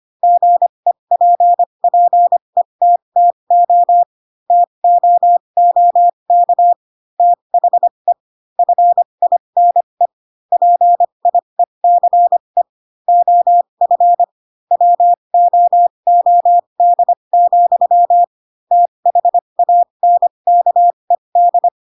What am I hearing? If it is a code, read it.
Morse code, heard as a keyed tone, about 25 wpm, spaced as for 22 wpm, 700 Hz. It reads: GEPPETTO TOOK THE FINE PIECE OF WOOD, THANKED
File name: Pinocchio / Pinocchio ch2 18 25 ik1hge